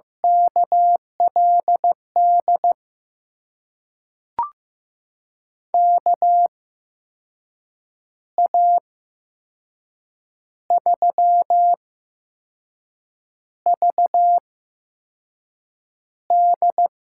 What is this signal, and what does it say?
Morse code, heard as a keyed tone, about 15 wpm, spaced as for 5 wpm, 700 Hz. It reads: KLD KA3VD
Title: USA Call Signs - 6 Characters - 8x Character Spacing - 15wpm